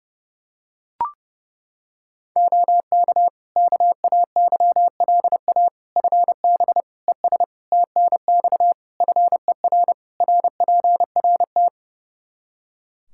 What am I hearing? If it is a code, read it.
Morse code, heard as a keyed tone, about 30 wpm, 700 Hz. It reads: OK KAYLA FB ES TNX FER RPRT
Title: K1USN SST QSO Practice 30wpm